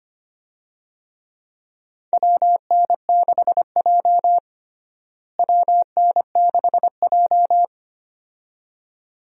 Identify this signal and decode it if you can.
Morse code, heard as a keyed tone, about 25 wpm, 700 Hz. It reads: WN6J WN6J
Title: USA Call Signs - 4 Characters - Repeated 2x - 25wpm